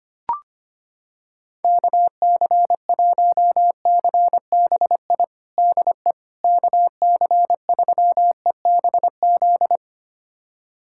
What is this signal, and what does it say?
Morse code, heard as a keyed tone, about 25 wpm, 700 Hz. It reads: KC1CBI DE KC3EBZ
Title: USA Callsign DE Callsign - 6 Characters - 25wpm